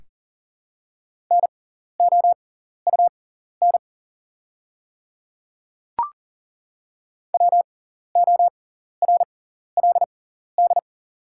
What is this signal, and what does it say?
Morse code, heard as a keyed tone, about 40 wpm, spaced as for 14 wpm, 700 Hz. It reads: NOUN WORLD
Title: 6x Character Spacing - Top 500 Words - 40wpm